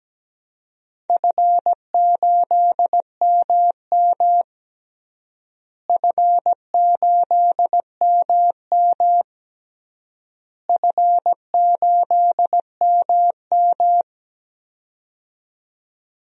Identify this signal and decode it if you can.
Morse code, heard as a keyed tone, about 17 wpm, 700 Hz. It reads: F8MM F8MM F8MM
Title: DXCC-100 Call Signs - 4 Characters - Repeated 3x - 17wpm